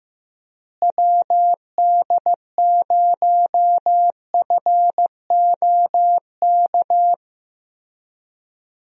Morse code, heard as a keyed tone, about 15 wpm, 700 Hz. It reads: WD0FOK
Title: USA Call Signs - 6 Characters - Repeated 3x - 15wpm